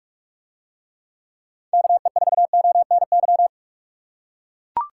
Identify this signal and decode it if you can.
Morse code, heard as a keyed tone, about 44 wpm, 700 Hz. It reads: KE4ONY